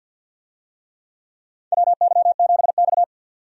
Morse code, heard as a keyed tone, about 49 wpm, 700 Hz. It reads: WY7X
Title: USA Call Signs - 4 Characters - 50wpm